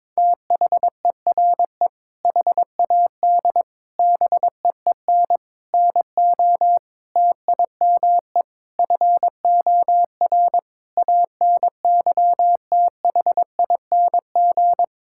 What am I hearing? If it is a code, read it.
Morse code, heard as a keyed tone, about 21 wpm, 700 Hz. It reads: THERE HAD BEEN NO TIME FOR ANYTHING
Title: Sentences from Top 1600 Words 22wpm